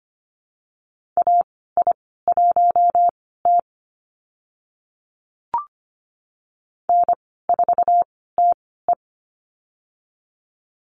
Morse code, heard as a keyed tone, about 25 wpm, spaced as for 16 wpm, 700 Hz. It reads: AI1T N4TE
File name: Lesson 15g - Number 5 - Call Signs 25wpm